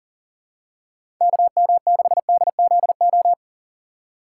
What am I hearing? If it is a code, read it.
Morse code, heard as a keyed tone, about 39 wpm, 690 Hz. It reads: KM6DZO
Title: USA Call Signs - 2x Character Spacing - 40wpm